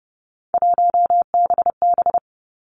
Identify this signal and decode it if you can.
Morse code, heard as a keyed tone, about 31 wpm, 700 Hz. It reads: 1BB
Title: Three Challenging Letter-Number Combinations 30wpm